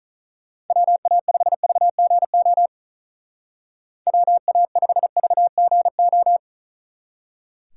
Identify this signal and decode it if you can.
Morse code, heard as a keyed tone, about 43 wpm, 700 Hz. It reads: WA5VGO WA5VGO
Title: Speed Racing - USA 2x3 Call Signs 35wpm